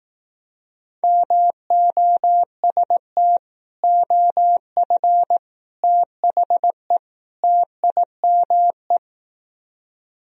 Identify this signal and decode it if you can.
Morse code, heard as a keyed tone, about 18 wpm, 700 Hz. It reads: MOST OF THE TIME
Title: Speed-Racing - Sets of 4 Words 15wpm